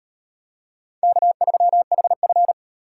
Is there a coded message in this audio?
Yes: Morse code from 1.0 to 2.5 s, heard as a keyed tone, about 36 wpm, 700 Hz. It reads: K3HF